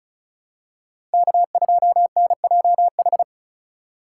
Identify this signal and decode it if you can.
Morse code, heard as a keyed tone, about 36 wpm, 700 Hz. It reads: K2NJH
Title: USA Call Signs - 5 Characters - 4x Character Spacing - 35wpm